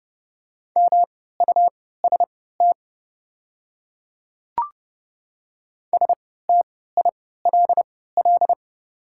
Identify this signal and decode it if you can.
Morse code, heard as a keyed tone, about 29 wpm, spaced as for 17 wpm, 700 Hz. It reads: MUST STILL
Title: 3x Character Spacing - Top 300 Words - 30wpm